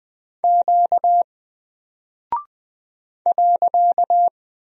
Morse code, heard as a keyed tone, about 20 wpm, 700 Hz. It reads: Q .
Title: Single Character (Letter:Number:Punctuation:Prosign) 20wpm